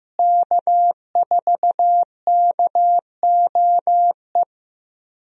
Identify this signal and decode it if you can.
Morse code, heard as a keyed tone, about 15 wpm, 700 Hz. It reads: K4KOE
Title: USA Call Signs - 5 Characters - Repeated 3x - 15wpm